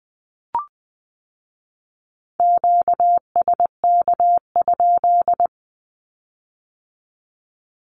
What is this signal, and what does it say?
Morse code, heard as a keyed tone, about 20 wpm, 700 Hz. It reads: QSK?
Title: Q-Codes with Spoken Meaning - 20wpm